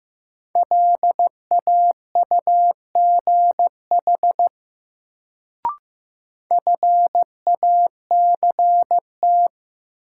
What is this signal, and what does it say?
Morse code, heard as a keyed tone, about 15 wpm, 700 Hz. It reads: LAUGH FACT
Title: Sentences from Top 500 Words - New Word Drill - 15wpm